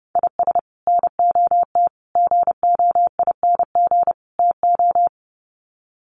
Morse code, heard as a keyed tone, about 30 wpm, 700 Hz. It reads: IS NOT GOING TO
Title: Sets of 4 Words 30wpm